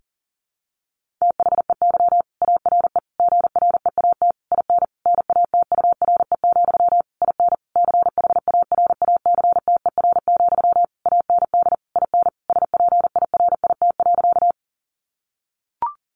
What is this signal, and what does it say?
Morse code, heard as a keyed tone, about 40 wpm, 700 Hz. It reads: THEY ARE GREAT IN NATURE, IN CHARACTER, AND IN SPIRIT.
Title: Sentences from Top 1400 Words 40wpm